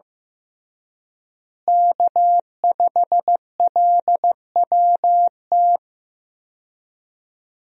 Morse code, heard as a keyed tone, about 15 wpm, 700 Hz. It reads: K5LWT
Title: USA Call Signs - 5 Characters - Repeated 3x - 15wpm